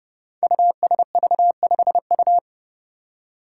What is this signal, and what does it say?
Morse code, heard as a keyed tone, about 30 wpm, 700 Hz. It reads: USV5U
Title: Five Challenging Letter-Number Combinations 30wpm